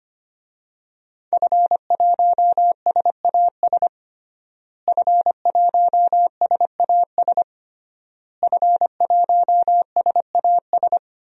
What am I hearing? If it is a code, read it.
Morse code, heard as a keyed tone, about 25 wpm, 700 Hz. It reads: F1SAS F1SAS F1SAS
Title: DXCC-100 Call Signs - 5 Characters - Repeated 3x - 25wpm